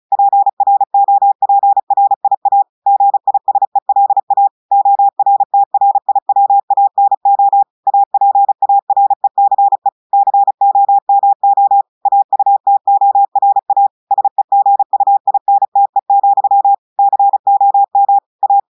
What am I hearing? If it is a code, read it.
Morse code, heard as a keyed tone, about 36 wpm, 800 Hz. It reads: PROPRIA GISELA ORTRIWANO APARECE COMO AUTORA SEGUINTE, COM A
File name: audios / portugais / portugais texte en clair 1 vitesse 35 mots minutes